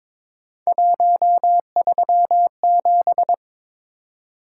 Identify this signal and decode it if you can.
Morse code, heard as a keyed tone, about 22 wpm, 700 Hz. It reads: 137
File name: practice / 22wpm / Three Letter-Number Combinations - 22wpm